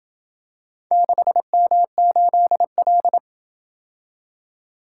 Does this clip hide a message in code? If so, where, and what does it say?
Yes: Morse code from 0.9 to 3.2 s, heard as a keyed tone, about 27 wpm, 700 Hz. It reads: 6M8L